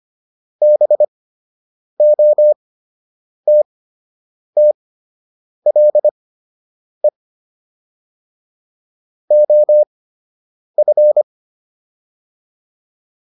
Morse code, heard as a keyed tone, about 25 wpm, spaced as for 8 wpm, 600 Hz. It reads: BOTTLE OF